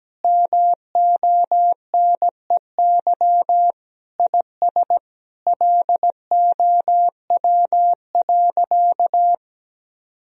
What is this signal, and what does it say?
Morse code, heard as a keyed tone, about 17 wpm, 700 Hz. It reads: MONEY IS LOW.
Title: Sentences from Top 1000 Words - Review Part 1 - 17wpm